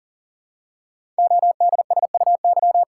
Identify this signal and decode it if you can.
Morse code, heard as a keyed tone, about 38 wpm, 700 Hz. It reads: ODSUY
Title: Code Groups - 5 Letters - 40wpm